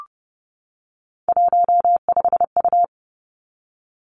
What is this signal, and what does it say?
Morse code, heard as a keyed tone, about 30 wpm, 700 Hz. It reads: 15U
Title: Three Challenging Letter-Number Combinations 30wpm